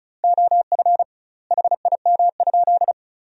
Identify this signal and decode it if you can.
Morse code, heard as a keyed tone, about 36 wpm, 700 Hz. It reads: OF HIM?
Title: 2x Word-Spacing - Sentences from Top 100 Words - 35wpm